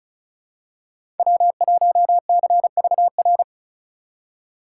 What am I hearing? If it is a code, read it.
Morse code, heard as a keyed tone, about 36 wpm, 700 Hz. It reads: W1CVR
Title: USA Call Signs - 5 Characters - 4x Character Spacing - 35wpm